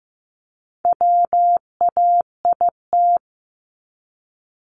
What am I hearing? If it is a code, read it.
Morse code, heard as a keyed tone, about 15 wpm, 700 Hz. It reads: WAIT
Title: ING Suffix - Top 100 Words with Base Words 15wpm